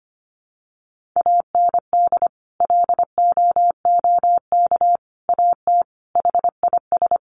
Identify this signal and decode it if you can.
Morse code, heard as a keyed tone, about 25 wpm, 700 Hz. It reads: AND LOOK AT HIS
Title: Sentences from Top 1500 Words 25wpm